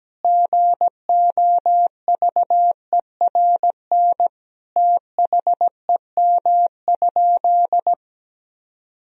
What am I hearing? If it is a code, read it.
Morse code, heard as a keyed tone, about 17 wpm, 700 Hz. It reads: GOVERN THEM?
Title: Sentences from Top 1000 Words - Review Part 2 - 17wpm